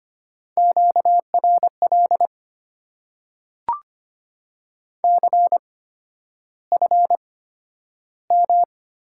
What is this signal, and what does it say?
Morse code, heard as a keyed tone, about 25 wpm, spaced as for 11 wpm, 700 Hz. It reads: QRL CFM
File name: Top 90 QSO Elements - 8x Character Spacing - 25wpm